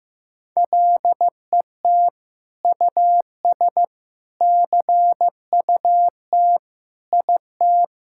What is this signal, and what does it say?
Morse code, heard as a keyed tone, about 16 wpm, 700 Hz. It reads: LET US CUT IT
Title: Sentences from Top 1000 Words - Review Part 6 - 15wpm